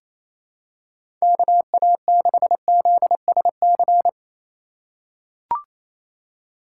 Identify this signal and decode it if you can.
Morse code, heard as a keyed tone, about 28 wpm, 700 Hz. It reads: KA6ZSC
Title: USA Call Signs - 6 Characters - 4x Character Spacing - 28wpm